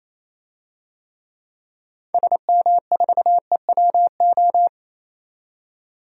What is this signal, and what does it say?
Morse code, heard as a keyed tone, about 28 wpm, 700 Hz. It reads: SM4EWO
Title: DXCC-100 Call Signs - 6 Characters - 28wpm